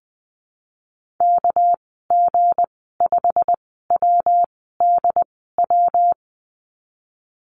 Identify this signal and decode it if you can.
Morse code, heard as a keyed tone, about 19 wpm, spaced as for 14 wpm, 700 Hz. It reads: KG5WDW
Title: 238 Lesson 38g - Letter q - Call Signs 20wpm